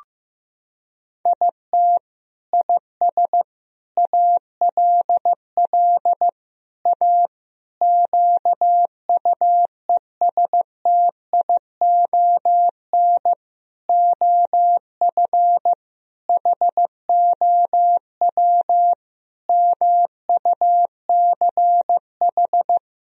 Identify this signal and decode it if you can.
Morse code, heard as a keyed tone, about 15 wpm, 700 Hz. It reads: IT IS ALL A QUESTION OF HOW MUCH
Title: Sentences from Top 1000 Words - Review Part 3 - 15wpm